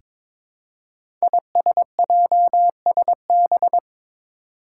Morse code, heard as a keyed tone, about 22 wpm, 700 Hz. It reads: ISJSB
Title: Five Challenging Letter-Number Combinations 22wpm